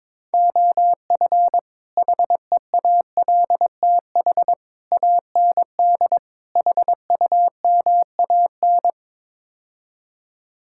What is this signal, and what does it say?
Morse code, heard as a keyed tone, about 22 wpm, 700 Hz. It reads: OF HEALTH AND HUMAN
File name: practice / 22wpm / Build A Phrase - 5-Word Phrases - 22wpm